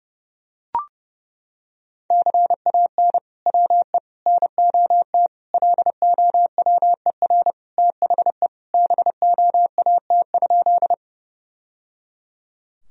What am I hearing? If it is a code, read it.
Morse code, heard as a keyed tone, about 30 wpm, 700 Hz. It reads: CAN WE NOT LOWER THE BOAT?
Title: Sentences from Top 1000 Words - Review Part 4 - 30wpm